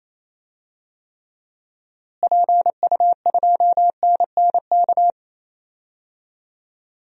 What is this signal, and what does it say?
Morse code, heard as a keyed tone, about 28 wpm, 700 Hz. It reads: PU2NNK
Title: DXCC-100 Call Signs - 6 Characters - Repeated 2x - 28wpm